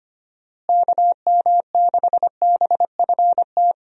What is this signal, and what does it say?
Morse code, heard as a keyed tone, about 25 wpm, 700 Hz. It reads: KM6BFT
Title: USA Call Signs - 6 Characters - Repeated 3x - 25wpm